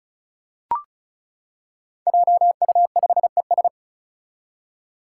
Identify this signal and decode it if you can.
Morse code, heard as a keyed tone, about 36 wpm, 700 Hz. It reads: JU5ES
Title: Five Challenging Letter-Number Combinations 35wpm